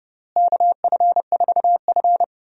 Morse code, heard as a keyed tone, about 29 wpm, 700 Hz. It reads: KF4F
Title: USA Call Signs - 4 Characters - Repeated 3x - 30wpm